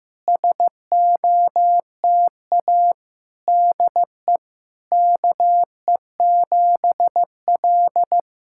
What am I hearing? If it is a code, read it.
Morse code, heard as a keyed tone, about 16 wpm, 700 Hz. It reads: SOTA DE KE7L
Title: SOTA QSO Practice 15wpm